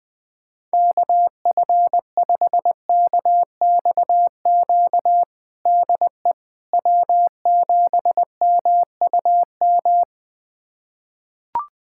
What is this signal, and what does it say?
Morse code, heard as a keyed tone, about 21 wpm, 700 Hz. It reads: KF5KXQ DE W7MUM
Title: USA Callsign DE Callsign 20wpm